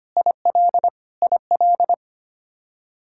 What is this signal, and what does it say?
Morse code, heard as a keyed tone, about 25 wpm, 700 Hz. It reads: IL IL
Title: K1USN SST QSO Practice 25wpm